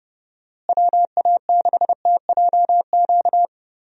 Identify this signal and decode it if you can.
Morse code, heard as a keyed tone, about 30 wpm, 700 Hz. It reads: WA6TJQ
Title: USA Call Signs - 6 Characters - Repeated 3x - 30wpm